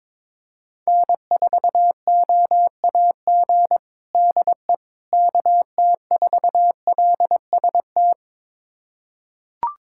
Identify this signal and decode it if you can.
Morse code, heard as a keyed tone, about 22 wpm, 700 Hz. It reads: N4OAG DE KT4LST
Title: USA Callsign DE Callsign 22wpm